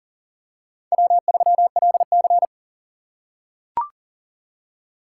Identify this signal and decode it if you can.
Morse code, heard as a keyed tone, about 40 wpm, 700 Hz. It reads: W3LC
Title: USA Call Signs - 4 Characters - 4x Character Spacing - 40wpm